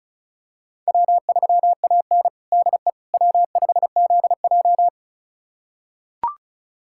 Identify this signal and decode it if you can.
Morse code, heard as a keyed tone, about 34 wpm, 700 Hz. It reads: W3AN DE W5ZJ
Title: USA Callsign DE Callsign - 4 Characters - 35wpm